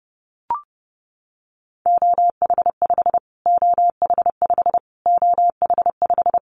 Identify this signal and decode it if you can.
Morse code, heard as a keyed tone, about 30 wpm, 700 Hz. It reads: OH5 OH5 OH5
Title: Callsign Prefixes for Top 25 DXCC - Repeated 3x - 30wpm